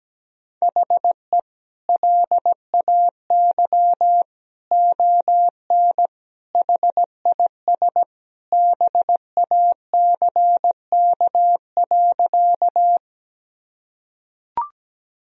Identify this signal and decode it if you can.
Morse code, heard as a keyed tone, about 17 wpm, 700 Hz. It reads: HE LAY ON HIS BACK.
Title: Sentences from Top 1000 Words - Review Part 2 - 17wpm